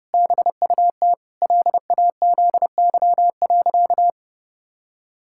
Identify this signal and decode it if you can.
Morse code, heard as a keyed tone, about 30 wpm, 700 Hz. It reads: BUT LAZY.
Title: Pangram Challenge v2 - 30wpm